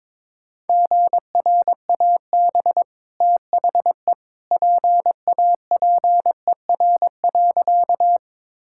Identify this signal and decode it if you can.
Morse code, heard as a keyed tone, about 22 wpm, 690 Hz. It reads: GRAB THE PAPER.
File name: practice / 22wpm / Easy Sentences with Hints - 22wpm